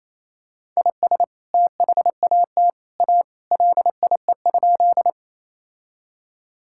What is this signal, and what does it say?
Morse code, heard as a keyed tone, about 27 wpm, 700 Hz. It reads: IS THAT A LIE?
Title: Sentences from Top 1600 Words 28wpm